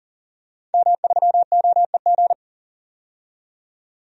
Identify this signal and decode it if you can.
Morse code, heard as a keyed tone, about 37 wpm, 700 Hz. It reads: M3OEG
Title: DXCC-100 Call Signs - 5 Characters - 40wpm